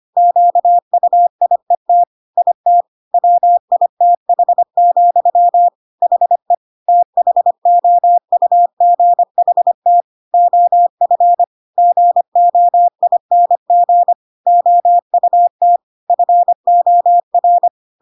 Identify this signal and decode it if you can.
Morse code, heard as a keyed tone, about 25 wpm, 700 Hz. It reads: QUIET IT WITH, HE THOUGHT OF GOING OUT FOR
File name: Pinocchio / Pinocchio ch5 25 25 ik1hge